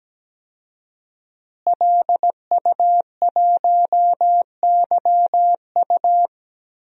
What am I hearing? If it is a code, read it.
Morse code, heard as a keyed tone, about 17 wpm, 700 Hz. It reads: LU1YU
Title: DXCC-100 Call Signs - 5 Characters - Repeated 2x - 17wpm